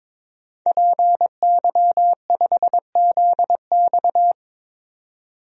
Morse code, heard as a keyed tone, about 22 wpm, 700 Hz. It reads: PY5ZX
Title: DXCC-100 Call Signs - 5 Characters - 22wpm